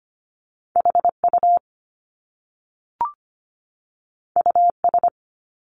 Morse code, heard as a keyed tone, about 25 wpm, 700 Hz. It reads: HU US
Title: Top Two Letter Combinations in Words 25wpm